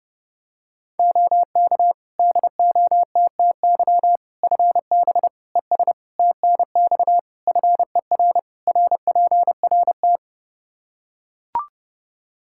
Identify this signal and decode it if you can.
Morse code, heard as a keyed tone, about 30 wpm, 700 Hz. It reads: OK DOTTY FB ES TNX FER RPRT
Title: K1USN SST QSO Practice 30wpm